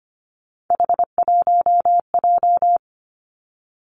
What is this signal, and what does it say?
Morse code, heard as a keyed tone, about 25 wpm, 700 Hz. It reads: H1J
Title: Three Challenging Letter-Number Combinations 25wpm